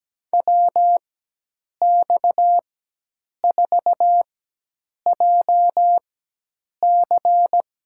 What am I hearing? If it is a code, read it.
Morse code, heard as a keyed tone, about 17 wpm, spaced as for 8 wpm, 700 Hz. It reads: WX4JC
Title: USA Call Signs - 5 Characters - 4x Character Spacing - 17wpm